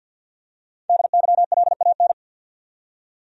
Morse code, heard as a keyed tone, about 52 wpm, 700 Hz. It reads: DYLAN
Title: USA Common Names - Max 5 Letters - 50wpm